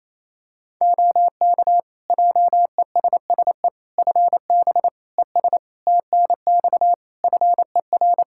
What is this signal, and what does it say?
Morse code, heard as a keyed tone, about 27 wpm, 700 Hz. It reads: OK JESSE FB ES TNX FER
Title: Practice QSO - Exchange 2 - Info - 28wpm